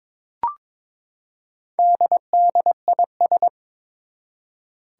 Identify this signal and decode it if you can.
Morse code, heard as a keyed tone, about 22 wpm, 700 Hz. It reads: DDIS
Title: Four Challenging Letter-Number Combinations 22wpm